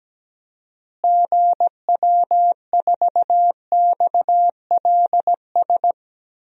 Morse code, heard as a keyed tone, about 17 wpm, 700 Hz. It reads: GW4XLS
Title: DXCC-100 Call Signs - 6 Characters - Repeated 3x - 17wpm